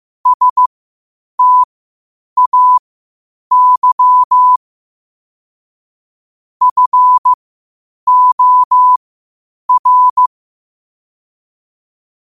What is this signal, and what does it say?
Morse code, heard as a keyed tone, about 15 wpm, spaced as for 8 wpm, 1000 Hz. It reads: STAY FOR
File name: qotdfiles / 8wpm